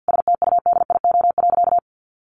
Morse code, heard as a keyed tone, about 50 wpm, 700 Hz. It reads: STUDIO.
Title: Sentences from Top 1200 Words 50wpm